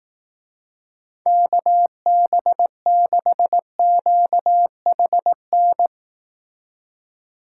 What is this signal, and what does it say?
Morse code, heard as a keyed tone, about 18 wpm, 700 Hz. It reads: KB6QHN
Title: Speed Racing - USA 2x3 Call Signs 15wpm